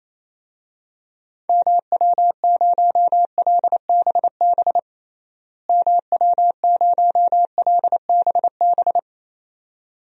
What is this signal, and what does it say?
Morse code, heard as a keyed tone, about 28 wpm, 700 Hz. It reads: MW0LBB MW0LBB
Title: DXCC-100 Call Signs - 6 Characters - Repeated 2x - 28wpm